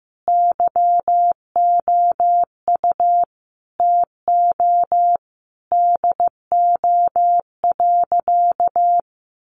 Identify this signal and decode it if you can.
Morse code, heard as a keyed tone, about 15 wpm, 700 Hz. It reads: YOU TO DO.
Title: Sentences from Top 1300 Words 15wpm